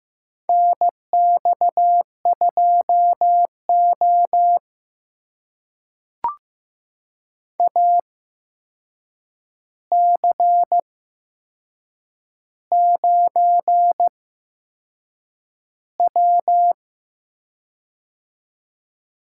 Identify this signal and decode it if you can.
Morse code, heard as a keyed tone, about 15 wpm, spaced as for 6 wpm, 700 Hz. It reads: NX2O AC9W
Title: USA Call Signs - 4 Characters - 8x Character Spacing - 15wpm